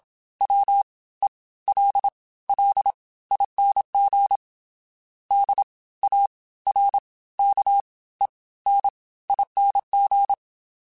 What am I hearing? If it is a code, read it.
Morse code, heard as a keyed tone, about 27 wpm, spaced as for 17 wpm, 800 Hz. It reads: WELLING DARKENING
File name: Special INGs / Special INGs-mp3s / Fast ing @25